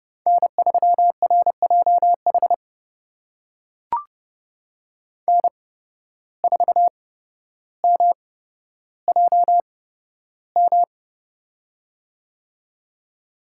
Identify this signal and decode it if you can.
Morse code, heard as a keyed tone, about 30 wpm, spaced as for 13 wpm, 700 Hz. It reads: N3RJH N4MJM
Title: USA Call Signs - 5 Characters - 8x Character Spacing - 30wpm